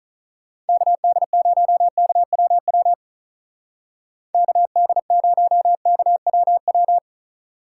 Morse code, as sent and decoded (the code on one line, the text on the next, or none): -.- -.. ----- -.- .-- .-- / -.- -.. ----- -.- .-- .--
KD0KWW KD0KWW